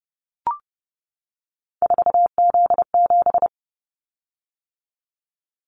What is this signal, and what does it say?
Morse code, heard as a keyed tone, about 30 wpm, 700 Hz. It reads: 4Z7
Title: Callsign Prefixes for Top 50 DXCC - 30wpm